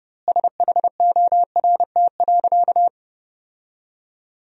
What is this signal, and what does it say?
Morse code, heard as a keyed tone, about 30 wpm, 700 Hz. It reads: SHORT.
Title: 2x Word-Spacing - Sentences from Top 500 Words - 30wpm